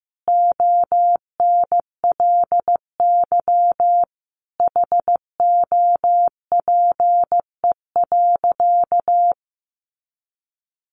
Morse code, heard as a keyed tone, about 15 wpm, 700 Hz. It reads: ONLY HOPE.
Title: Sentences from Top 1000 Words 15wpm